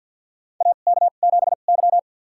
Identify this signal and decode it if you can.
Morse code, heard as a keyed tone, about 53 wpm, spaced as for 37 wpm, 700 Hz. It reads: AK7Y
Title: USA Call Signs - 4 Characters - 2x Character Spacing - 50wpm